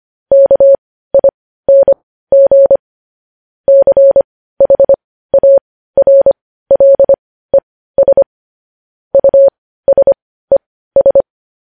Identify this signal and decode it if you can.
Morse code, heard as a keyed tone, about 25 wpm, spaced as for 15 wpm, 550 Hz. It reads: KING CHARLES USES
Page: news headlines in morse code at 15 wpm